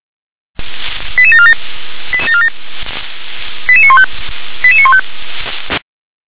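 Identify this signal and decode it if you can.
Five-tone selective call (ZVEI), heard as quick runs of five tones, tones of about 70 ms. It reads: 88747 88915 88915
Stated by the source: ZVEI five-tone call 88742 received by SDR.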